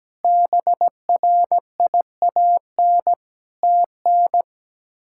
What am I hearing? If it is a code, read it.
Morse code, heard as a keyed tone, about 17 wpm, 700 Hz. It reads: BRIAN TN
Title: K1USN SST QSO Practice - 17wpm